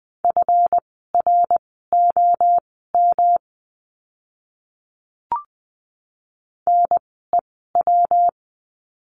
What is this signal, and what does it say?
Morse code, heard as a keyed tone, about 20 wpm, spaced as for 14 wpm, 700 Hz. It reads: FROM NEW
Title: 206 Lesson 33f - Letter j - Words 20wpm